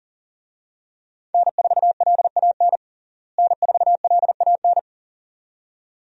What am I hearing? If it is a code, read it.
Morse code, heard as a keyed tone, about 40 wpm, 700 Hz. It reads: N4LAN N4LAN
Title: USA Call Signs - 5 Characters - Repeated 2x - 40wpm